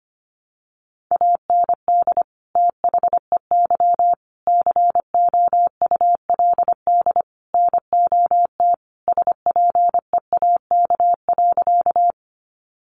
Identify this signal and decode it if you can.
Morse code, heard as a keyed tone, about 25 wpm, 700 Hz. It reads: AND THEY COULD NOT SPEAK.
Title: Sentences from Top 1100 Words 25wpm